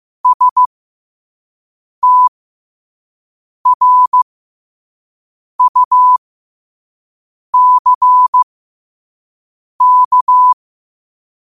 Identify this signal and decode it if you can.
Morse code, heard as a keyed tone, about 15 wpm, spaced as for 5 wpm, 1000 Hz. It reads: STRUCK
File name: qotdfiles / 5wpm